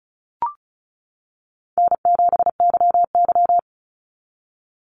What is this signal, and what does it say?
Morse code, heard as a keyed tone, about 34 wpm, 700 Hz. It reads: N7YY
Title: NA SOTA Chaser Callsigns 35wpm